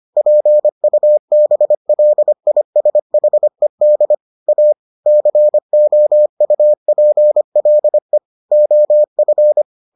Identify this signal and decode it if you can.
Morse code, heard as a keyed tone, about 25 wpm, 600 Hz. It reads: PUBLISHED A COUPLE OF